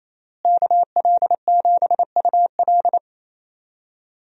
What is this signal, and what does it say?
Morse code, heard as a keyed tone, about 28 wpm, 700 Hz. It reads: KL7UL
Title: USA Call Signs - 5 Characters - Repeated 3x - 28wpm